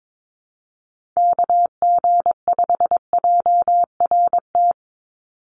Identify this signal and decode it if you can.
Morse code, heard as a keyed tone, about 22 wpm, 700 Hz. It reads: KG5JRT
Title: USA Call Signs 22wpm